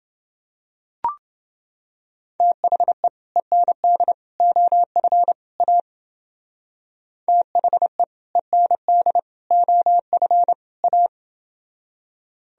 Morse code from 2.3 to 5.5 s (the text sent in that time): THE END OF